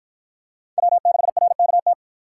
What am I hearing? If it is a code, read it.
Morse code, heard as a keyed tone, about 48 wpm, 700 Hz. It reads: W6RKT